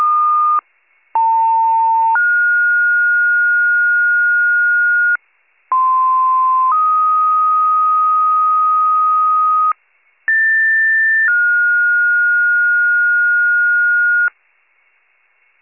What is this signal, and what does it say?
Two-tone page, heard as a one-second tone, then a three-second tone, about 900 then 1430 Hz, about 1040 then 1250 Hz, about 1800 then 1430 Hz, with noise. It is heard from 1.1 to 14.3 s.